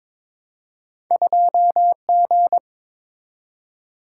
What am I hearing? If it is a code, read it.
Morse code, heard as a keyed tone, about 22 wpm, 700 Hz. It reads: 2G
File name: Two Letter-Number Combinations - 22wpm